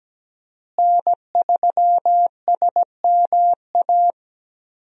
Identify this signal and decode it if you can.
Morse code, heard as a keyed tone, about 17 wpm, 700 Hz. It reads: N3SMA
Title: USA Call Signs - 5 Characters - Repeated 2x - 17wpm